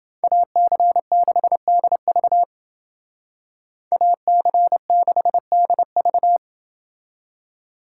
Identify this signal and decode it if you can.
Morse code, heard as a keyed tone, about 28 wpm, 700 Hz. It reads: AC6DV AC6DV